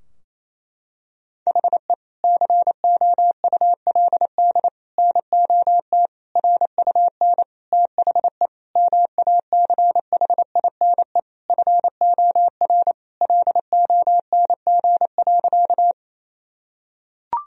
Morse code, heard as a keyed tone, about 28 wpm, 700 Hz. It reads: HE COULD NOT RUN THE MACHINE FOR LONG.
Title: Sentences from Top 1000 Words - Review Part 2 - 28wpm